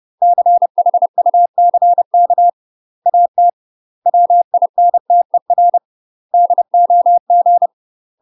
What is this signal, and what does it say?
Morse code, heard as a keyed tone, about 30 wpm, 700 Hz. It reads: CHUCK AT WINTER DOG
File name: UK MASTER QSOs TRAINING/30WPM